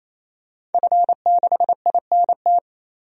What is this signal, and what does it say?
Morse code, heard as a keyed tone, about 28 wpm, 700 Hz. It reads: F6INT
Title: DXCC-100 Call Signs - 5 Characters - 28wpm